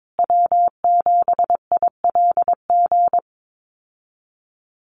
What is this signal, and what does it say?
Morse code, heard as a keyed tone, about 22 wpm, 700 Hz. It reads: W7ILG
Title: USA Call Signs Repeated 3x 22wpm